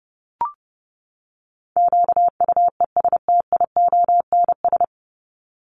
Morse code, heard as a keyed tone, about 30 wpm, 700 Hz. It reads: QUESTIONS
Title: ES:S Suffix - Top 100 Words with Base Words 30wpm